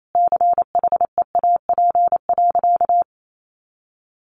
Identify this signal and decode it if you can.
Morse code, heard as a keyed tone, about 28 wpm, 700 Hz. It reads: CHEAP.
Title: Sentences from Top 1200 Words 28wpm